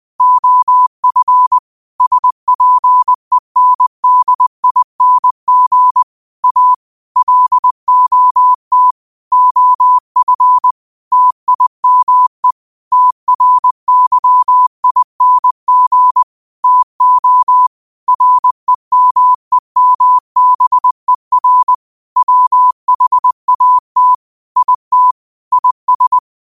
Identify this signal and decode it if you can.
Morse code, heard as a keyed tone, about 20 wpm, 1000 Hz. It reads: OF SPENDING A LOT OF TIME TRYING TO REMEMBER WHAT IT IS
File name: qotdfiles / 20wpm